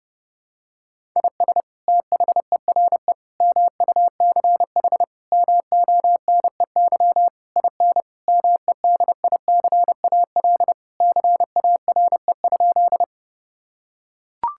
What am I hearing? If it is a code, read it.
Morse code, heard as a keyed tone, about 30 wpm, 700 Hz. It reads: IS THERE MUCH MONEY IN MEDICAL CARE?
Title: Sentences from Top 1000 Words - Review Part 5 - 30wpm